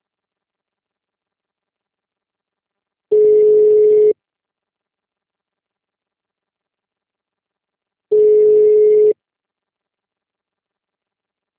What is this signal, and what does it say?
Ringback tone (Europe, 425 Hz).
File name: out- -2015- - - .38801